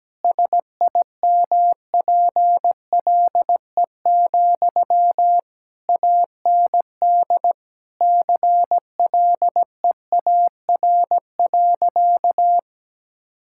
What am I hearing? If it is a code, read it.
Morse code, heard as a keyed tone, about 17 wpm, 700 Hz. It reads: SIMPLE, AND CLEAR.
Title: Sentences from Top 1000 Words - Review Part 6 - 17wpm